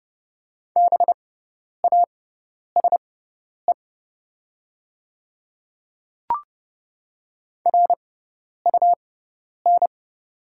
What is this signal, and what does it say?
Morse code, heard as a keyed tone, about 29 wpm, spaced as for 10 wpm, 690 Hz. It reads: BASE RUN
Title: 6x Character Spacing - Top 300 Words - 30wpm